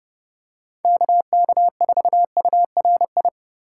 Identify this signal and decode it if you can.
Morse code, heard as a keyed tone, about 30 wpm, 700 Hz. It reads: KK4URI
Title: USA Call Signs - 6 Characters - 4x Character Spacing - 30wpm